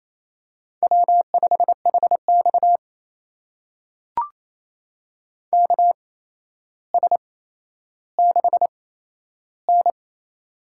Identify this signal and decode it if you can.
Morse code, heard as a keyed tone, about 28 wpm, spaced as for 12 wpm, 700 Hz. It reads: W5HX KS6N
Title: USA Call Signs - 8x Character Spacing - 28wpm